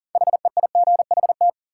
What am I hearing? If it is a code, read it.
Morse code, heard as a keyed tone, about 40 wpm, 700 Hz. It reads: HEIGHT
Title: Sentences from Top 1600 Words - New Word Drill - 40wpm